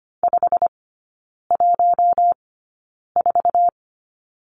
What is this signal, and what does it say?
Morse code, heard as a keyed tone, about 25 wpm, spaced as for 9 wpm, 700 Hz. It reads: 514
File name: Lesson 15e - Number 5 - Three Characters 25wpm